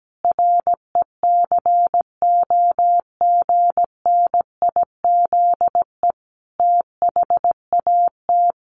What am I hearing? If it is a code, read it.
Morse code, heard as a keyed tone, about 17 wpm, 700 Hz. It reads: RECOGNIZE THAT
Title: Sentences from Top 1400 Words 17wpm